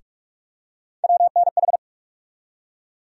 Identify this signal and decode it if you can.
Morse code, heard as a keyed tone, about 46 wpm, 700 Hz. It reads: WNH
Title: USA Call Sign Postfixes - 3 Letters 45wpm